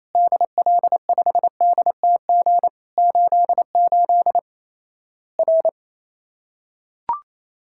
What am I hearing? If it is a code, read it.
Morse code, heard as a keyed tone, about 28 wpm, 700 Hz. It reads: DL5DTG 88 R
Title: Worked All Europe DX Contest - QTC Exchange - 28wpm